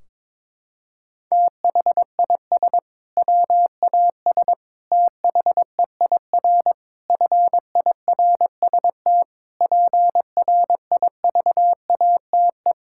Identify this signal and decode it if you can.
Morse code, heard as a keyed tone, about 22 wpm, 700 Hz. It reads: THIS WAS THEIR FIRST PRIVATE
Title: Sentences from Top 1600 Words 22wpm